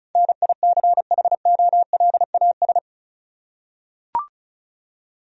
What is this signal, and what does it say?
Morse code, heard as a keyed tone, about 34 wpm, 700 Hz. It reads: NICHOLAS
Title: USA Common Names - 35wpm